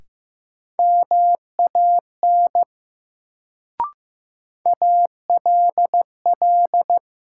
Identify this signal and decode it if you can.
Morse code, heard as a keyed tone, about 15 wpm, 700 Hz. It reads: MAN ALL
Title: Sets of 2 Words - New Word Drill - 15wpm